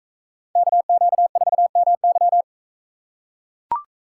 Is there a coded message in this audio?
Yes: Morse code from 0.6 to 2.4 s, heard as a keyed tone, about 44 wpm, 700 Hz. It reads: KQ4MY